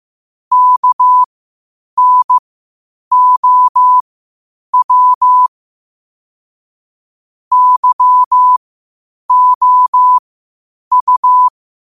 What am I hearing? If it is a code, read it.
Morse code, heard as a keyed tone, about 15 wpm, spaced as for 8 wpm, 1000 Hz. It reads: KNOW YOU
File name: qotdfiles / 8wpm